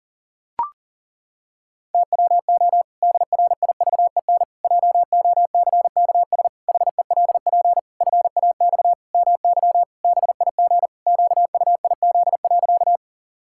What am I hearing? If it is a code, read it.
Morse code, heard as a keyed tone, about 40 wpm, 700 Hz. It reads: TWO DRIVEN JOCKS HELP FAX MY BIG QUIZ.
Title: Pangram Challenge v2 - 40wpm